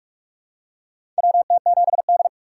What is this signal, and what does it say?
Morse code, heard as a keyed tone, about 46 wpm, 700 Hz. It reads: WT7D
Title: USA Call Signs - 4 Characters - 2x Character Spacing - 45wpm